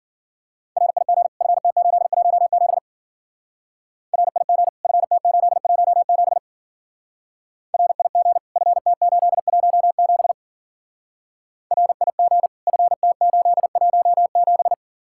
Morse code, heard as a keyed tone, about 50 wpm, 700 Hz. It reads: RIG FT817 RIG FT817 RIG FT817 RIG FT817
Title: Speed-Racing Common QSO Phrases - 40wpm